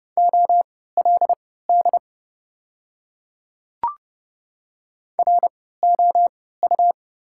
Morse code, heard as a keyed tone, about 32 wpm, spaced as for 17 wpm, 700 Hz. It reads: OLD ROU
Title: 3x Character Spacing - Top 200 Words - 30wpm